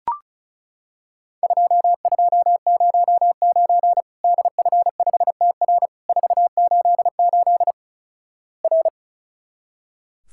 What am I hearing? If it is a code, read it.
Morse code, heard as a keyed tone, about 35 wpm, 700 Hz. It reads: 2209 DF5TR 488 R
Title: Worked All Europe DX Contest - QTC Exchange - 35wpm